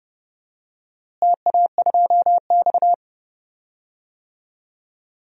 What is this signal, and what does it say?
Morse code, heard as a keyed tone, about 28 wpm, 700 Hz. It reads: TA2X
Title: DXCC-100 Call Signs - 4 Characters - Repeated 3x - 30wpm